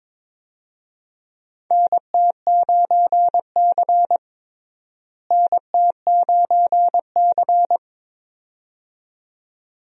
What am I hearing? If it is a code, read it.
Morse code, heard as a keyed tone, about 22 wpm, 700 Hz. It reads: NT9C NT9C
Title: USA Call Signs - 4 Characters - Repeated 2x - 22wpm